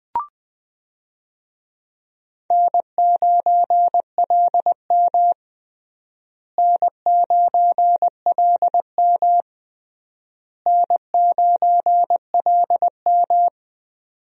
Morse code, heard as a keyed tone, about 20 wpm, 700 Hz. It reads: N9LM N9LM N9LM
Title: USA Call Signs - 4 Characters - Repeated 3x - 20wpm